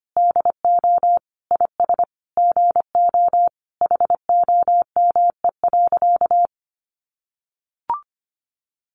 Morse code, heard as a keyed tone, about 25 wpm, 700 Hz. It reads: DO IS GO HOME.
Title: Sentences from Top 200 Words 25wpm